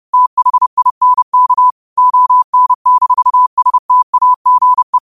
Morse code, heard as a keyed tone, about 30 wpm, 1000 Hz. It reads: THINK ON-STAGE